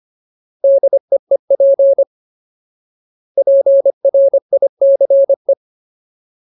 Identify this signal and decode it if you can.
Morse code, heard as a keyed tone, about 25 wpm, 550 Hz. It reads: DEEP PRICE
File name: Ultimate Word List / 25WPM Ultimate